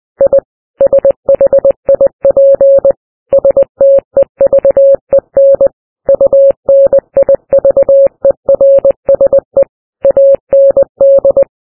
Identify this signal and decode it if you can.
Morse code, heard as a keyed tone, about 20 wpm, 550 Hz. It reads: I SHIP STEVEN UNIVERSE AND
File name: morse-ship